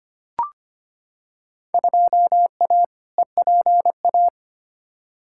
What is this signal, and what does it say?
Morse code, heard as a keyed tone, about 25 wpm, 700 Hz. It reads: 2A EPA
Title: Field Day Exchanges - 25wpm